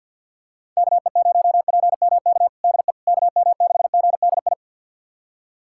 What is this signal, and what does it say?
Morse code, heard as a keyed tone, about 50 wpm, 700 Hz. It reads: KE0PMK DE KM6GDI
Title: USA Callsign DE Callsign - 6 Characters - 50wpm